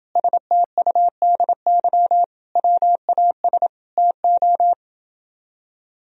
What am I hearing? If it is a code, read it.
Morse code, heard as a keyed tone, about 28 wpm, 700 Hz. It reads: STUDY WAS TO